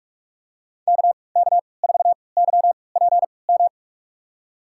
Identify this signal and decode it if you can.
Morse code, heard as a keyed tone, about 44 wpm, spaced as for 25 wpm, 700 Hz. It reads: KK4YPM
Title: USA Call Signs - 6 Characters - 3x Character Spacing - 45wpm